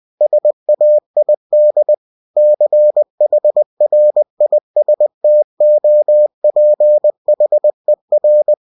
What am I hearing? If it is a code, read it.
Morse code, heard as a keyed tone, about 20 wpm, 600 Hz. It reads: SAID CHRISTOPHER
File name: cw-books / winnie-the-pooh / winnie-20wpm-page-0088